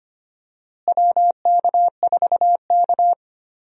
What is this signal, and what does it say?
Morse code, heard as a keyed tone, about 25 wpm, 700 Hz. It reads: WK4K
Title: USA Call Signs - 4 Characters - Repeated 3x - 25wpm